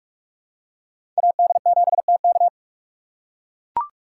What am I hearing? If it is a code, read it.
Morse code, heard as a keyed tone, about 44 wpm, 700 Hz. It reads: AD7TK
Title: USA Call Signs - 4x Character Spacing - 45wpm